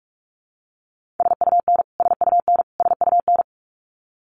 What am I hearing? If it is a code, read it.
Morse code, heard as a keyed tone, about 46 wpm, 700 Hz. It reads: SUN SUN SUN
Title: Top 300 Words Repeated 45wpm